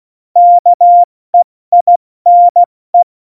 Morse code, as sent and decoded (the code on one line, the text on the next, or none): -.- . .. -. .
KEINE